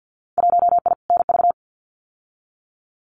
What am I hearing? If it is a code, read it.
Morse code, heard as a keyed tone, about 50 wpm, 700 Hz. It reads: 1I NV
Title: Winter Field Day Exchanges Encore 50wpm